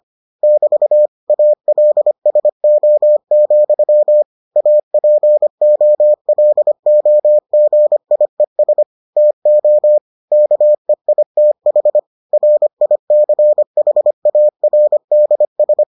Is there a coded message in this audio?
Yes: Morse code from 0.4 to 15.8 s, heard as a keyed tone, about 25 wpm, 600 Hz. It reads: <BT> ALSO, APOLOGIES TO KEITH RICHARDS